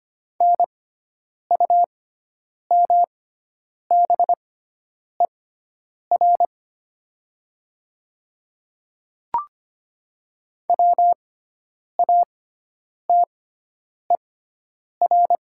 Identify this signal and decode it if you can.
Morse code, heard as a keyed tone, about 25 wpm, spaced as for 9 wpm, 700 Hz. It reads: NUMBER WATER
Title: 6x Character Spacing - Top 100 Words - 25wpm